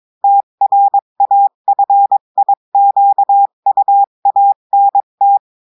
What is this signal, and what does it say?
Morse code, heard as a keyed tone, about 22 wpm, 800 Hz. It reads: TRAFIQUANT
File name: cours-du-mercredi / audios / cours-853-du-2-mai-2024-etxte chiffre et texte clair-a-20-wpm